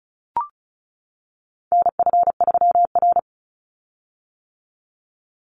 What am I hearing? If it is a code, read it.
Morse code, heard as a keyed tone, about 34 wpm, 700 Hz. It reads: NF3R
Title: Call Signs - 35wpm